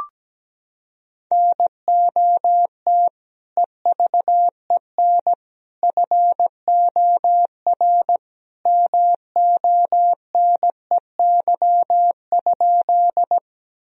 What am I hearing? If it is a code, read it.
Morse code, heard as a keyed tone, about 17 wpm, 700 Hz. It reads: NOT EVEN FOR MONEY?
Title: Sentences from Top 1000 Words - Review Part 1 - 17wpm